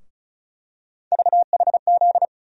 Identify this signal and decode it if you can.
Morse code, heard as a keyed tone, about 36 wpm, 700 Hz. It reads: VHZ
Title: USA Call Sign Postfixes - 3 Letters 35wpm